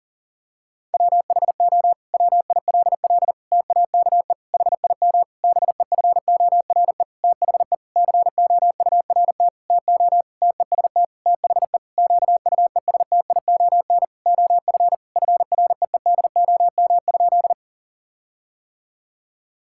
Morse code, heard as a keyed tone, about 40 wpm, 700 Hz. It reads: WHO WILL TAKE HIM BEFORE THE COURT TO TEST THE QUESTION OF FREEDOM?
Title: Sentences from Top 1600 Words 40wpm